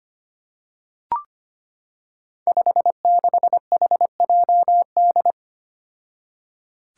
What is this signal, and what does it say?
Morse code, heard as a keyed tone, about 25 wpm, 700 Hz. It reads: 56HJD
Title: Five Challenging Letter-Number Combinations 25wpm